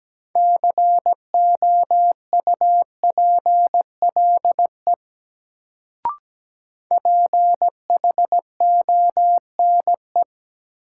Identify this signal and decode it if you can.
Morse code, heard as a keyed tone, about 17 wpm, 700 Hz. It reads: COUPLE PHONE
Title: Sentences from Top 1100 Words - New Word Drill Alt - 17wpm